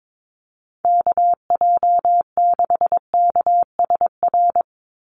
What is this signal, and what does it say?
Morse code, heard as a keyed tone, about 22 wpm, 700 Hz. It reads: KJ6KSR
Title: USA Call Signs 22wpm